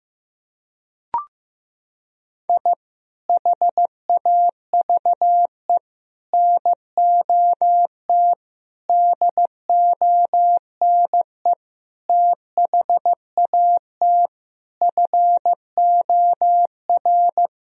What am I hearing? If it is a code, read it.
Morse code, heard as a keyed tone, about 15 wpm, 700 Hz. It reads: I HAVE NOT DONE THAT FOR
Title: Sentences from Top 1000 Words - Review Part 1 - 15wpm